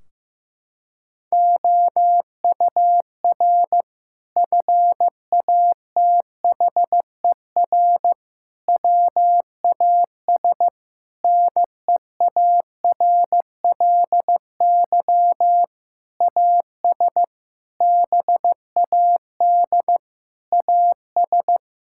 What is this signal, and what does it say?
Morse code, heard as a keyed tone, about 15 wpm, 700 Hz. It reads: OUR FATHER WAS NEARLY AS BAD AS
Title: Sentences from Top 1600 Words 15wpm